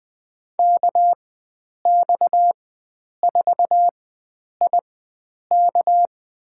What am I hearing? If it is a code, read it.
Morse code, heard as a keyed tone, about 20 wpm, spaced as for 9 wpm, 700 Hz. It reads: KX4IK
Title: USA Call Signs - 5 Characters - 4x Character Spacing - 20wpm